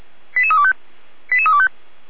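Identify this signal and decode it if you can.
Five-tone selective call (ZVEI), heard as quick runs of five tones, tones of about 70 ms. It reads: 88325 88325
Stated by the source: ZVEI five-tone selective call 88499 received on an SDR, recorded by feuerwehr-orlamuende.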